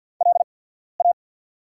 Morse code, heard as a keyed tone, about 48 wpm, spaced as for 14 wpm, 700 Hz. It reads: LA
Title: 8x Character Spacing - Top 300 Words - 50wpm